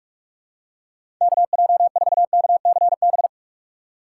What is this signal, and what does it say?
Morse code, heard as a keyed tone, about 47 wpm, 700 Hz. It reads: KJ4KCB
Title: USA Call Signs - 6 Characters - 8x Character Spacing - 45wpm